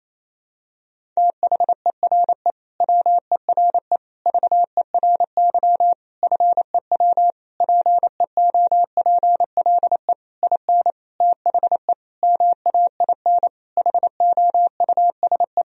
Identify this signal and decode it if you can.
Morse code, heard as a keyed tone, about 27 wpm, 700 Hz. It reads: THERE WERE VERY FEW PEOPLE IN THE MAIN HOUSE
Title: Sentences from Top 1000 Words - Review Part 2 - 28wpm